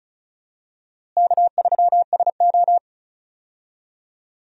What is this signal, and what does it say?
Morse code, heard as a keyed tone, about 36 wpm, 700 Hz. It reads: K3SO
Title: USA Call Signs - 4 Characters - Repeated 3x - 35wpm